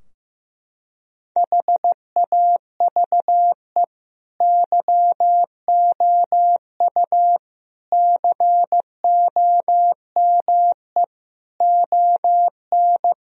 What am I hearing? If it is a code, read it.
Morse code, heard as a keyed tone, about 16 wpm, 700 Hz. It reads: HAVE YOU COME ON
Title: Sentences from Top 1000 Words - Review Part 3 - 15wpm